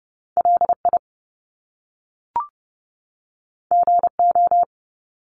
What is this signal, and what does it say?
Morse code, heard as a keyed tone, about 30 wpm, 700 Hz. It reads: LI GO
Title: Top Two Letter Combinations in Words 30wpm